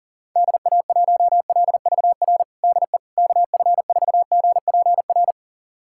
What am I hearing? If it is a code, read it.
Morse code, heard as a keyed tone, about 40 wpm, 700 Hz. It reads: DA1LVR DE KF4GPR
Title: USA Callsign DE Callsign - 6 Characters - 40wpm